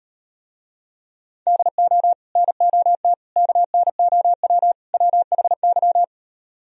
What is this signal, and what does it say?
Morse code, heard as a keyed tone, about 38 wpm, 700 Hz. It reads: DO NOT KNOW WHY